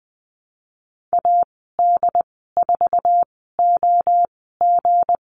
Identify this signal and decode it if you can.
Morse code, heard as a keyed tone, about 20 wpm, spaced as for 15 wpm, 700 Hz. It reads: AD4OG
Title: 173 Lesson 28g - Number 9 - Call Signs 20wpm